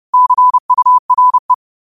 Morse code, heard as a keyed tone, about 30 wpm, 1000 Hz. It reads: CURE